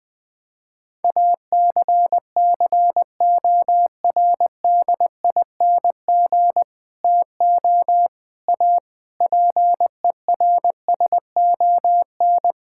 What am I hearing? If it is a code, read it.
Morse code, heard as a keyed tone, about 20 wpm, 700 Hz. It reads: ACCORDING TO A PERSON
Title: Sentences with 8-Word Phrases - 20wpm